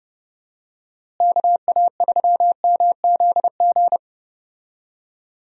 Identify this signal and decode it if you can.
Morse code, heard as a keyed tone, about 30 wpm, 700 Hz. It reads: KA3MZG
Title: USA Call Signs - 6 Characters - Repeated 3x - 30wpm